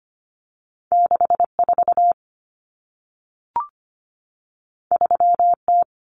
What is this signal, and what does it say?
Morse code, heard as a keyed tone, about 25 wpm, 700 Hz. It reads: 64 3T
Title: Common Cut Numbers - 2-Digit 25wpm